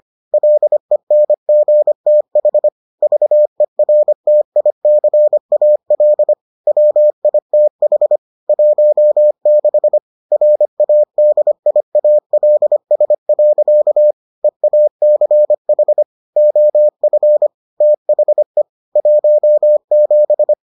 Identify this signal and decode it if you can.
Morse code, heard as a keyed tone, about 25 wpm, 600 Hz. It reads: LENGTH VERTICAL WITH 16 RADIALS. EACH OF THE 17